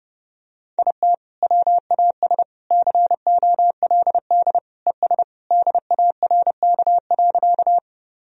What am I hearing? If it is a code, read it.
Morse code, heard as a keyed tone, about 29 wpm, 700 Hz. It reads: IT WAS COLD ES DARK.
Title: Sentences from Top 500 Words and Abbreviations 30wpm